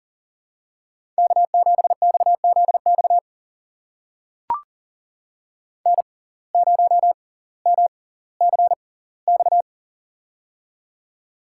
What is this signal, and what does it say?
Morse code, heard as a keyed tone, about 40 wpm, spaced as for 20 wpm, 700 Hz. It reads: K7XZX N0MCX
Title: USA Call Signs - 5 Characters - 6x Character Spacing - 40wpm